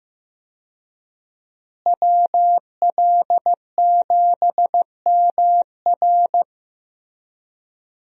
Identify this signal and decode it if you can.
Morse code, heard as a keyed tone, about 15 wpm, 700 Hz. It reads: WL7MR
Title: USA Call Signs - 5 Characters - Repeated 3x - 15wpm